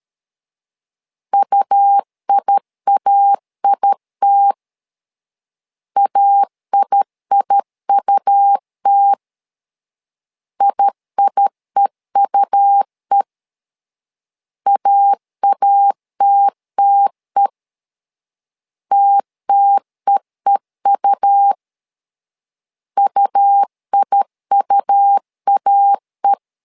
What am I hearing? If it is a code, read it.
Morse code, heard as a keyed tone, about 13 wpm, 800 Hz. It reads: UIAIT AIIUT IIEUE AATTE TTEEU UIUAE
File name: LEZIONI / 2 UHDN / 2 U 12wpm 1338